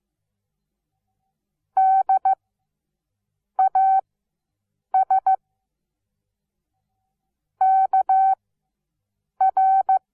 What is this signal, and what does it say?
Morse code, heard as a keyed tone, about 15 wpm, spaced as for 6 wpm, 760 Hz. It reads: DAS KR
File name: Onlinelehrgang / morsen / DJ4UF Morsen17